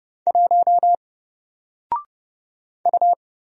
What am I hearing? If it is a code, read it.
Morse code, heard as a keyed tone, about 30 wpm, 700 Hz. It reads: 1 U
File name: Single-Letter-Number-30wpm